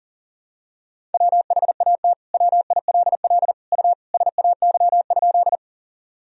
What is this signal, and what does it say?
Morse code, heard as a keyed tone, about 40 wpm, 700 Hz. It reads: WHAT WILL U SAY?
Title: Sentences from Top 200 Words and Abbreviations 40wpm